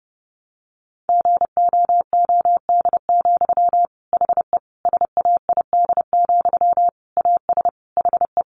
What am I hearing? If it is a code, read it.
Morse code, heard as a keyed tone, about 30 wpm, 700 Hz. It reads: GOOD, HE SAID, AS HE
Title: Sentences from Top 600 Words 30wpm